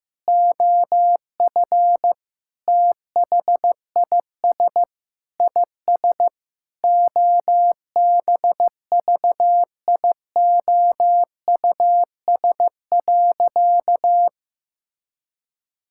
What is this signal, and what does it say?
Morse code, heard as a keyed tone, about 15 wpm, 700 Hz. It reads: OF THIS IS OBVIOUS.
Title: Sentences from Top 1600 Words 15wpm